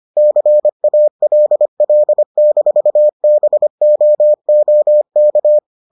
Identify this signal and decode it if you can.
Morse code, heard as a keyed tone, about 25 wpm, 600 Hz. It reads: CALL-BOOK